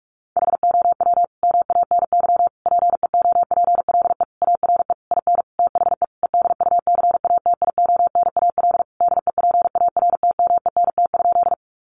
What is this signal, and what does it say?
Morse code, heard as a keyed tone, about 44 wpm, 700 Hz. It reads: HOW MANY PEOPLE ARE IN THE EDUCATIONAL DEPARTMENT?
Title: Sentences from Top 1200 Words 45wpm